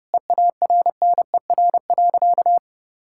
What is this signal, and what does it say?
Morse code, heard as a keyed tone, about 32 wpm, 700 Hz. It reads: EARNER.
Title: Pangram Challenge v2 - 30wpm